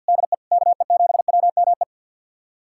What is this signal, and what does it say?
Morse code, heard as a keyed tone, about 51 wpm, 700 Hz. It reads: DE KE7WME
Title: USA Callsign DE Callsign - 6 Characters - 50wpm